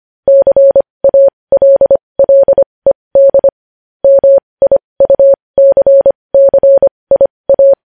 Morse code, heard as a keyed tone, about 25 wpm, spaced as for 20 wpm, 550 Hz. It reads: CALLED MIUCCIA
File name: podcast / News headlines-1x-25x20